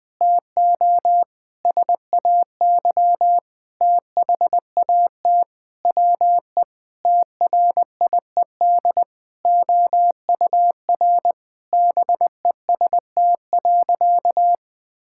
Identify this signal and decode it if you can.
Morse code, heard as a keyed tone, about 20 wpm, 700 Hz. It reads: TO SAY THAT WE TRIED OUR BEST.
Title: Sentences with 7-Word Phrases - Spoken Beginning - 20wpm